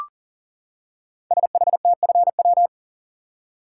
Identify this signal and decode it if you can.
Morse code, heard as a keyed tone, about 38 wpm, 700 Hz. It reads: SHTFW
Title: Code Groups - 5 Letters - 40wpm